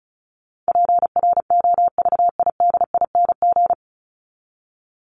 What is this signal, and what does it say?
Morse code, heard as a keyed tone, about 34 wpm, 700 Hz. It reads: PROVIDING
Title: ING Suffix - Top 100 Words with Base Words 35wpm